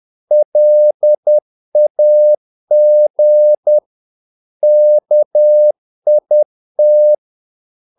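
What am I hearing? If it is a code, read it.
Morse code, heard as a keyed tone, about 10 wpm, 600 Hz. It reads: LAG KIT
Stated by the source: Morse practice of short Danish words at 10 wpm, from CW med Gnister.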